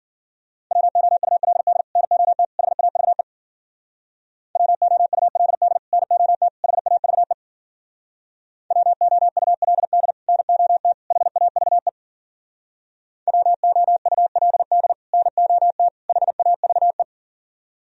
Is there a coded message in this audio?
Yes: Morse code from 0.7 to 17.0 s, heard as a keyed tone, about 51 wpm, 700 Hz. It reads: WOULD NOT HAVE WOULD NOT HAVE WOULD NOT HAVE WOULD NOT HAVE